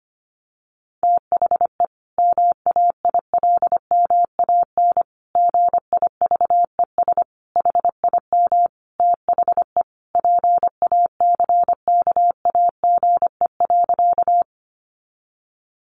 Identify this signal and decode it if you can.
Morse code, heard as a keyed tone, about 25 wpm, 700 Hz. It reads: THE MAILMAN GIVES HIM THE PACKAGE.
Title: Additional Easy Phrases 25wpm